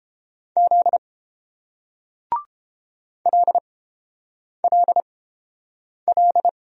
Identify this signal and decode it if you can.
Morse code, heard as a keyed tone, about 31 wpm, spaced as for 8 wpm, 700 Hz. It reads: Z LLL